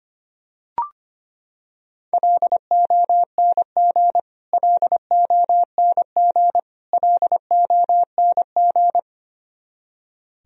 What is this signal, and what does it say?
Morse code, heard as a keyed tone, about 25 wpm, 700 Hz. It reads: LONG LONG LONG
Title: Top 100 Words Repeated 25wpm